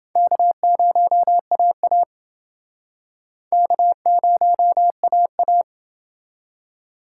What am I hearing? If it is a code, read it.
Morse code, heard as a keyed tone, about 28 wpm, 700 Hz. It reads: K0AA K0AA